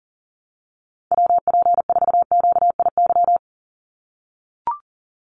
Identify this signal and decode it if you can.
Morse code, heard as a keyed tone, about 40 wpm, 700 Hz. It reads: WP4QIY
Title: USA Call Signs 40wpm